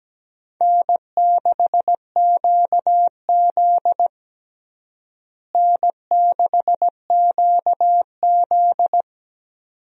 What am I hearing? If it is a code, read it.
Morse code, heard as a keyed tone, about 17 wpm, 700 Hz. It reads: N6QZ N6QZ
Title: USA Call Signs - 4 Characters - Repeated 2x - 17wpm